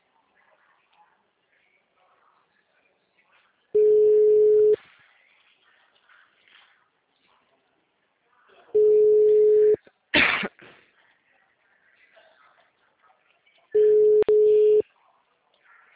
Ringback tone (Europe, 425 Hz).